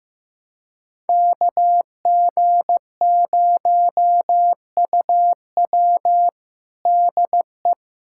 Morse code, heard as a keyed tone, about 15 wpm, 700 Hz. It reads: KG0UW DE
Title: USA Callsign DE Callsign - 5 Characters - 15wpm